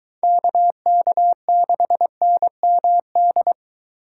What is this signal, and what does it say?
Morse code, heard as a keyed tone, about 23 wpm, 700 Hz. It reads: KK6NMD